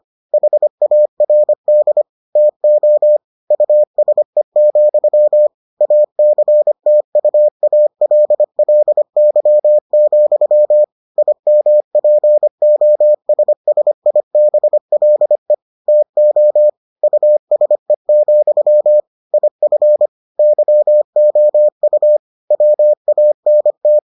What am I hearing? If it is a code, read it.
Morse code, heard as a keyed tone, about 25 wpm, 600 Hz. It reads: HARD TO USE, ACTUALLY, IMPOSSIBLE TO USE, IF YOU WANT